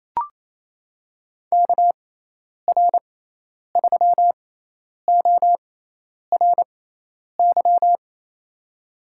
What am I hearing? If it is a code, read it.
Morse code, heard as a keyed tone, about 28 wpm, spaced as for 10 wpm, 700 Hz. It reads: KR3ORY
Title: USA Call Signs - 6 Characters - 6x Character Spacing - 28wpm